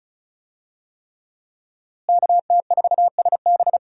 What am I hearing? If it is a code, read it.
Morse code, heard as a keyed tone, about 36 wpm, 700 Hz. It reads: KT4SB
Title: USA Call Signs - 5 Characters - Repeated 3x - 35wpm